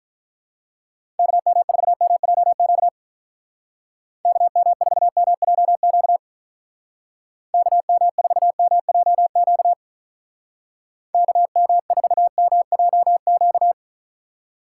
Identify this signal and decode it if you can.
Morse code, heard as a keyed tone, about 44 wpm, 700 Hz. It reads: KM4MJQ KM4MJQ KM4MJQ KM4MJQ
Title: Speed Racing - USA 2x3 Call Signs 35wpm